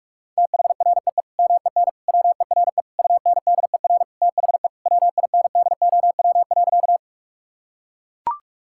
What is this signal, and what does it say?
Morse code, heard as a keyed tone, about 42 wpm, 700 Hz. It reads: THREE MEN WERE UNDER THE WINDOW.
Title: Sentences from Top 1600 Words 45wpm